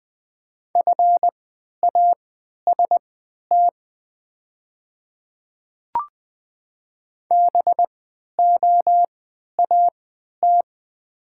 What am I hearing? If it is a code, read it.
Morse code, heard as a keyed tone, about 20 wpm, spaced as for 11 wpm, 700 Hz. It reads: FAST BOAT
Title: 3x Character Spacing - Top 500 Words - 20wpm